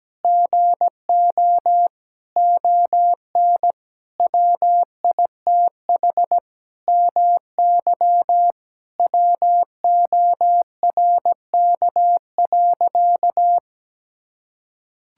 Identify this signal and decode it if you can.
Morse code, heard as a keyed tone, about 17 wpm, 700 Hz. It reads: GO ON WITH MY WORK.
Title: Sentences from Top 1000 Words - Review Part 6 - 17wpm